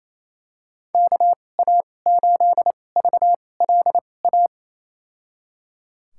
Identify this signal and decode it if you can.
Morse code, heard as a keyed tone, about 28 wpm, spaced as for 20 wpm, 700 Hz. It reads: KA8VLA
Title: USA Call Signs - 2x Character Spacing - 28wpm